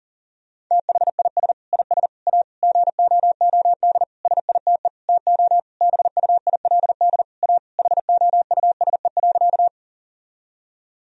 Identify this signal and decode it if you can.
Morse code, heard as a keyed tone, about 40 wpm, 700 Hz. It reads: THIS IS A GOOD SITE TO BUILD A HOUSE.
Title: Sentences from Top 1000 Words - Review Part 4 - 40wpm